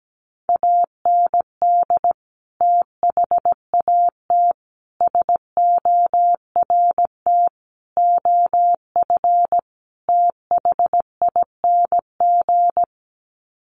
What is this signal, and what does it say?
Morse code, heard as a keyed tone, about 17 wpm, 700 Hz. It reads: AND THAT SORT OF THING
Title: Sets of 5 Words Encore2 17wpm